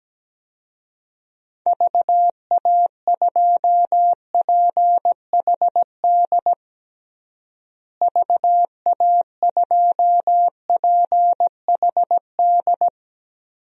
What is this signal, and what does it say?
Morse code, heard as a keyed tone, about 17 wpm, 700 Hz. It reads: VA2PHD VA2PHD
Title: DXCC-100 Call Signs - 6 Characters - Repeated 2x - 17wpm